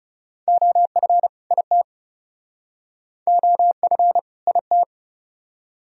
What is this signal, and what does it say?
Morse code, heard as a keyed tone, about 37 wpm, 700 Hz. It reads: OF IT OF IT
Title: Speed-Racing - Sets of 2 Words 30wpm